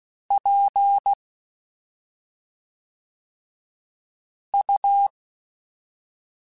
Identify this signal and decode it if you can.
Morse code, heard as a keyed tone, about 16 wpm, 800 Hz. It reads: P U
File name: Special INGs / Special INGs-mp3s / SlowToFaster ing @5